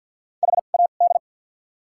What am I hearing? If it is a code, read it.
Morse code, heard as a keyed tone, about 48 wpm, 700 Hz. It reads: H A D
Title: 2x Character Spacing - Top 200 Words - 50wpm